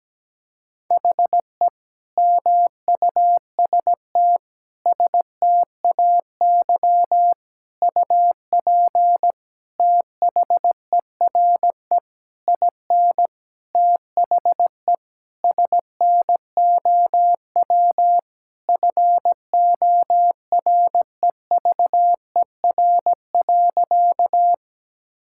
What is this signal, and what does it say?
Morse code, heard as a keyed tone, about 17 wpm, 700 Hz. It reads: HE MUST STAY UP THERE IN THE SNOW FOREVER.
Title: Sentences from Top 1600 Words 17wpm